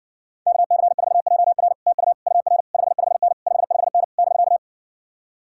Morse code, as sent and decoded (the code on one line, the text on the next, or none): -.- -.-. ....- .--- .-. / - ..- / ..- .-. / ..... ..... -. / ..... ..... -. / -...-.-
KC4JR TU UR 55N 55N <BK>